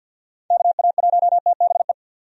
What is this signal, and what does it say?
Morse code, heard as a keyed tone, about 51 wpm, 700 Hz. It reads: KA1TBE
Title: USA Call Signs - 6 Characters - 50wpm